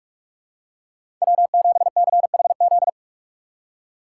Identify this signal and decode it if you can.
Morse code, heard as a keyed tone, about 46 wpm, 690 Hz. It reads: W7CHZ
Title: USA Call Signs - 5 Characters - 2x Character Spacing - 45wpm